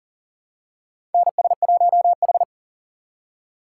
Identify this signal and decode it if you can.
Morse code, heard as a keyed tone, about 37 wpm, 700 Hz. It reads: NS1H
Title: USA Call Signs - 4 Characters - 4x Character Spacing - 40wpm